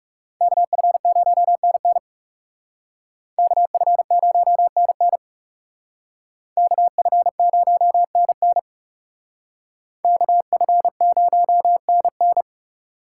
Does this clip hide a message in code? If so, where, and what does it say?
Yes: Morse code from 0.4 to 12.4 s, heard as a keyed tone, about 38 wpm, 700 Hz. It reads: KF0NN KF0NN KF0NN KF0NN